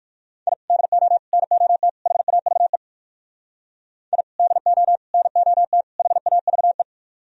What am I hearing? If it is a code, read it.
Morse code, heard as a keyed tone, about 53 wpm, 700 Hz. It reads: I DO NOT HAVE I DO NOT HAVE
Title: Speed-Racing - Sets of 4 Words 45wpm